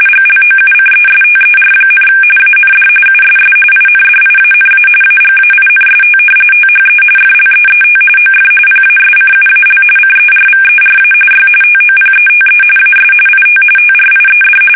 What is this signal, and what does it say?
RTTY, about 45.45 baud. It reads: GLOUCESTER, AND USS INDIANA AGAINST THE FLEET OF PASCUAL CERVERA Y TOPETE AS IT TRIED
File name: fsk850 / battleships